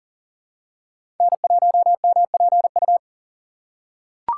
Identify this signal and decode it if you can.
Morse code, heard as a keyed tone, about 39 wpm, 700 Hz. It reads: N1MPU
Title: USA Call Signs - 5 Characters - 2x Character Spacing - 40wpm